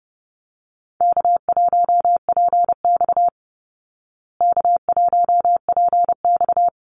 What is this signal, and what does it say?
Morse code, heard as a keyed tone, about 30 wpm, 700 Hz. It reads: K1PX K1PX
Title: Call Signs Repeated 3x - 30wpm